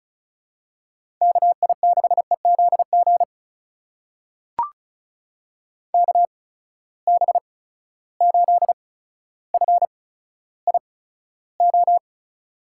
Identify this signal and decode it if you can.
Morse code, heard as a keyed tone, about 36 wpm, spaced as for 15 wpm, 700 Hz. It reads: KI6EZG KB8FIO
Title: USA Call Signs - 6 Characters - 8x Character Spacing - 35wpm